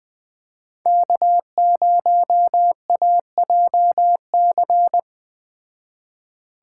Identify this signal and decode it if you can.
Morse code, heard as a keyed tone, about 20 wpm, 700 Hz. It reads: K0AJC
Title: Speed Racing - USA 1x3 Call Signs 17wpm